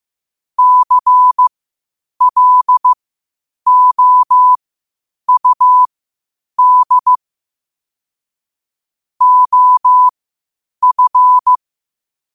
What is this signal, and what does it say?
Morse code, heard as a keyed tone, about 15 wpm, spaced as for 8 wpm, 1000 Hz. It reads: CLOUD OF
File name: qotdfiles / 8wpm